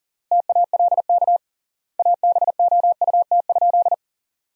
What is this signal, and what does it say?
Morse code, heard as a keyed tone, about 40 wpm, 700 Hz. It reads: TALK ABOUT?
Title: 3x Word-Spacing - Sentences from Top 400 Words - 40wpm